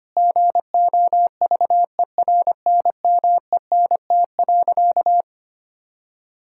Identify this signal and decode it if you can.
Morse code, heard as a keyed tone, about 25 wpm, 700 Hz. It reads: GOVERNMENT.
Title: Sentences from Top 1000 Words - Review Part 5 - 25wpm